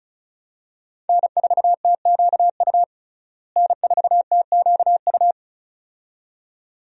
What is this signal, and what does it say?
Morse code, heard as a keyed tone, about 34 wpm, 700 Hz. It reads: N4TQU N4TQU
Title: USA Call Signs - 5 Characters - Repeated 2x - 35wpm